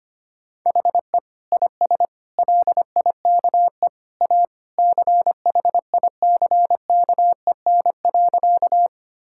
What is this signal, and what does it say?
Morse code, heard as a keyed tone, about 25 wpm, 700 Hz. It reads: HE IS LIKE A CHICKEN.
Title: Sentences from Top 1000 Words - Review Part 5 - 25wpm